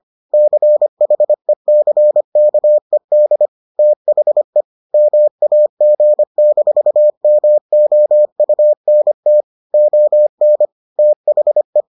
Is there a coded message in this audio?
Yes: Morse code from 0.3 to 11.8 s, heard as a keyed tone, about 25 wpm, 600 Hz. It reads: CHECKED THE MAG-MOUNT ON THE